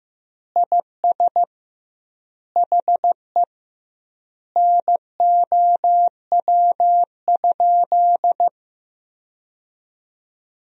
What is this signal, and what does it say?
Morse code, heard as a keyed tone, about 15 wpm, 700 Hz. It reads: IS HE NOW?
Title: 2x Word-Spacing - Sentences from Top 200 Words - 15wpm